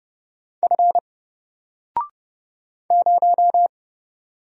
Morse code, heard as a keyed tone, about 30 wpm, 700 Hz. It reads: F 0
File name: Single-Letter-Number-30wpm